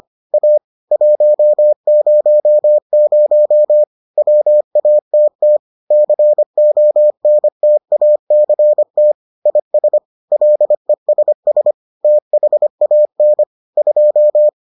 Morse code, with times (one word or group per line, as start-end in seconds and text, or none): 0.3-0.6 A
0.9-3.8 100
4.2-5.6 WATT
5.9-9.1 CONTACT
9.5-10.0 IS
10.3-11.7 LESS
12.0-13.4 THAN
13.8-14.5 2